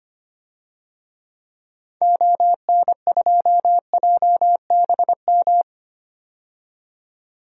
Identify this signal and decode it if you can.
Morse code, heard as a keyed tone, about 25 wpm, 700 Hz. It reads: ON2JBM
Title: DXCC-100 Call Signs - 6 Characters - 25wpm